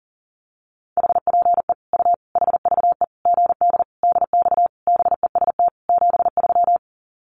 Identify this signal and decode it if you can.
Morse code, heard as a keyed tone, about 40 wpm, 700 Hz. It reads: HPE U HVE GD DX BEST 73
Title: Common QSO Phrases 40wpm